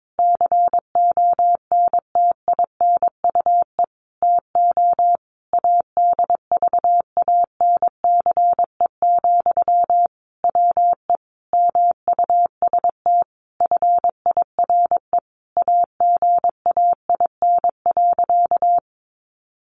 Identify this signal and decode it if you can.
Morse code, heard as a keyed tone, about 22 wpm, 700 Hz. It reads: CONTINUE TO ADVANCE, WE MUST FIRE AGAIN.
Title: Sentences from Top 1400 Words 22wpm